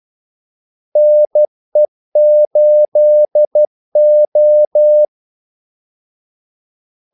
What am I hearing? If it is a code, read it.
Morse code, heard as a keyed tone, about 12 wpm, 600 Hz. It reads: NE8O
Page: 2 ekstra ordmellemrum mellem kaldesignalerne